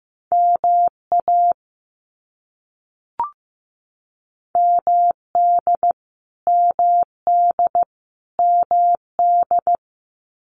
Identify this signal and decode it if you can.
Morse code, heard as a keyed tone, about 15 wpm, 700 Hz. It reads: MA MD MD MD
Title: US States Repeated 3x 15wpm